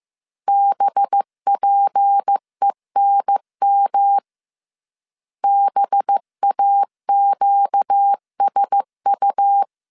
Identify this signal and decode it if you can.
Morse code, heard as a keyed tone, about 15 wpm, 800 Hz. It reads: BPENM BAQSU